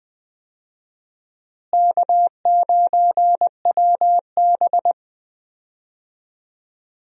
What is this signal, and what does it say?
Morse code, heard as a keyed tone, about 20 wpm, 700 Hz. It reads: K9WB
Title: USA Call Signs - 4 Characters - Repeated 2x - 20wpm